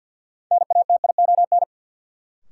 Morse code, heard as a keyed tone, about 52 wpm, 700 Hz. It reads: NATION
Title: Sets of 3 Words - New Word Drill - 50wpm